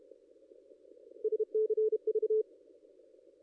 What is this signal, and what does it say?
Morse code, heard as a keyed tone, about 33 wpm, 430 Hz. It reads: SCV
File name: contests / ss-cw-k5tr-op-n5rz -2200